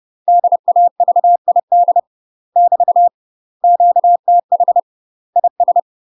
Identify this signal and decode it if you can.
Morse code, heard as a keyed tone, about 30 wpm, 700 Hz. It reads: DAVID <BT> QTH IS
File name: UK MASTER QSOs TRAINING / 30WPM / UK MASTER QSOs 0040